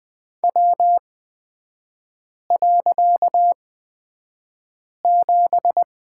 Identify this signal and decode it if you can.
Morse code, heard as a keyed tone, about 20 wpm, spaced as for 5 wpm, 700 Hz. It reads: W.7
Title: 255 Lesson 41e - Period - Three Characters 20wpm